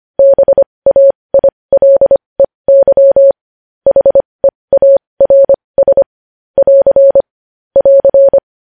Morse code, heard as a keyed tone, about 25 wpm, spaced as for 20 wpm, 550 Hz. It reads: BAILEY HEARS <AR> <AR>
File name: podcast / News headlines-1x-25x20